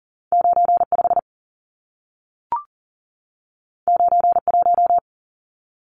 Morse code, heard as a keyed tone, about 40 wpm, 700 Hz. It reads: 95 91
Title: Numbers - 2-Digit 40wpm